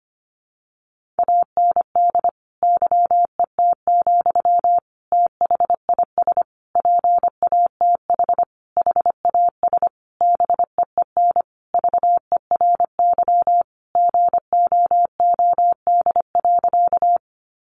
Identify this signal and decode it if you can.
Morse code, heard as a keyed tone, about 25 wpm, 700 Hz. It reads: AND YET, THIS PATH HAS BEEN VERY GOOD.
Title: Sentences from Top 1400 Words 25wpm